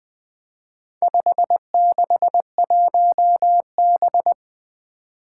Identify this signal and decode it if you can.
Morse code, heard as a keyed tone, about 20 wpm, 700 Hz. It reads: 561B
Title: Four Challenging Letter-Number Combinations 20wpm